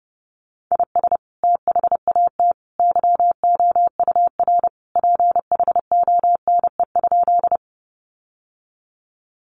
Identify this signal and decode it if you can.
Morse code, heard as a keyed tone, about 30 wpm, 700 Hz. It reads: IS THAT YOUR PHONE?
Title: Sentences from Top 1100 Words 30wpm